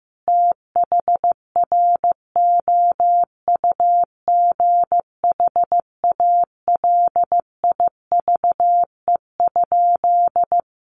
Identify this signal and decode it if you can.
Morse code, heard as a keyed tone, about 15 wpm, 700 Hz. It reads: THROUGH ALIVE?
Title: A Sound Of Thunder 15wpm